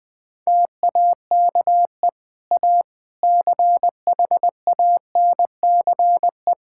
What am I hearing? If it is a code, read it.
Morse code, heard as a keyed tone, about 20 wpm, 700 Hz. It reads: TAKE A CHANCE
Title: Sentences with 3-Word Phrases - 20wpm